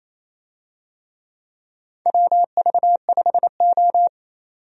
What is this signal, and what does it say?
Morse code, heard as a keyed tone, about 27 wpm, 700 Hz. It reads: WV5O
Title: USA Call Signs - 4 Characters - 28wpm